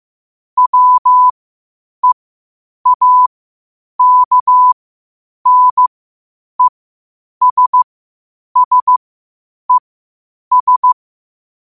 Morse code, heard as a keyed tone, about 15 wpm, spaced as for 8 wpm, 1000 Hz. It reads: WEAKNESSES
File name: 8wpm